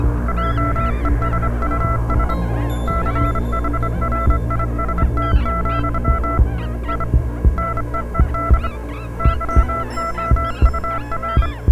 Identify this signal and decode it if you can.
Morse code, heard as a keyed tone, about 25 wpm, 1340 Hz, with noise. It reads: JESUS CHRIST, I NEED YOUR